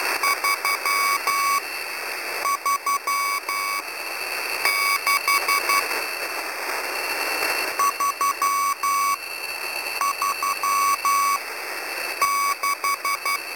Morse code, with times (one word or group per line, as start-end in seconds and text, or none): 0.2-5.8 335
7.8-13.4 336